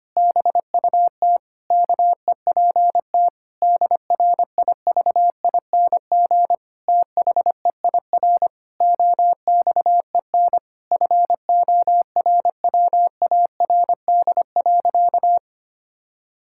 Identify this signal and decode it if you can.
Morse code, heard as a keyed tone, about 25 wpm, 700 Hz. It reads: BUT KEPT DRIVING THEIR OXEN FORWARD.
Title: Pangram Challenge v2 - 25wpm